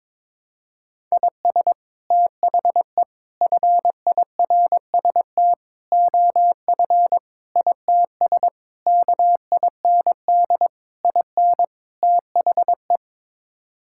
Morse code, heard as a keyed tone, about 22 wpm, 700 Hz. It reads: IS THE FIRST OF ITS KIND IN THE
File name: practice / 22wpm / Sets of 8 Words - 22wpm